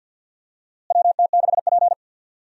Morse code, heard as a keyed tone, about 51 wpm, 700 Hz. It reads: WT6P
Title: USA Call Signs - 4 Characters - Repeated 2x - 50wpm